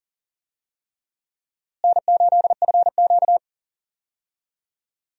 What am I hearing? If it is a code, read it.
Morse code, heard as a keyed tone, about 40 wpm, 700 Hz. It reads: N8FQ
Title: USA Call Signs - 4 Characters - Repeated 3x - 40wpm